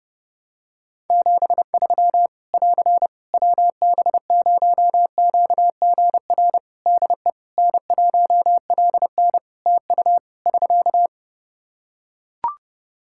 Morse code, heard as a keyed tone, about 30 wpm, 700 Hz. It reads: 73 <AR> WB0QGR DE N1LN TU <SK>
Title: Practice QSO - Exchange 4 - Ending - 30wpm